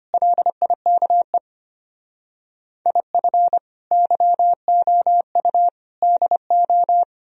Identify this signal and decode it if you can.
Morse code, heard as a keyed tone, about 32 wpm, 700 Hz. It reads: LIKE IF YOU DO